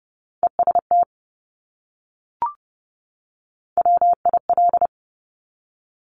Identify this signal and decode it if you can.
Morse code, heard as a keyed tone, about 30 wpm, 690 Hz. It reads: EST WIL
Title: Top Three Letter Combinations in Words 30wpm